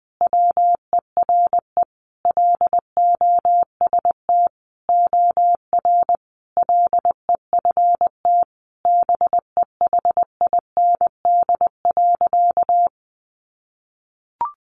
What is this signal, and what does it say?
Morse code, heard as a keyed tone, about 20 wpm, 700 Hz. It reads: WERE LOST OR LEFT BEHIND.
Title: Sentences from Top 1400 Words 20wpm